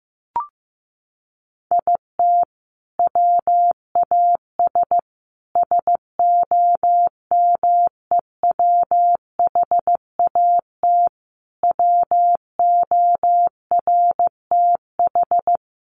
Morse code, heard as a keyed tone, about 14 wpm, 700 Hz. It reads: IT WAS SOMEWHAT WORTH
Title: Sentences from Top 1300 Words 15wpm